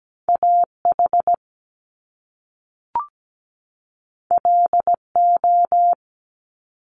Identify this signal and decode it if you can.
Morse code, heard as a keyed tone, about 17 wpm, 700 Hz. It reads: AH LO
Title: Top Two Letter Combinations in Words 17wpm